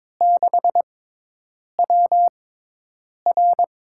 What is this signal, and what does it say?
Morse code, heard as a keyed tone, about 22 wpm, spaced as for 8 wpm, 700 Hz. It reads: 6WR
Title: USA Call Signs - 5 Characters - 6x Character Spacing - 22wpm